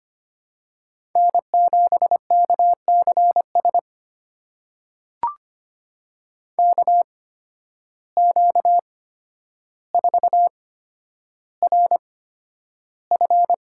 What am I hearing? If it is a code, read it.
Morse code, heard as a keyed tone, about 25 wpm, spaced as for 11 wpm, 700 Hz. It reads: N7KCS KQ4RF
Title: USA Call Signs - 5 Characters - 8x Character Spacing - 25wpm